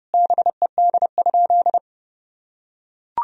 Morse code, heard as a keyed tone, about 30 wpm, 700 Hz. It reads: BED?
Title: Sentences from Top 1000 Words - Review Part 4 - 30wpm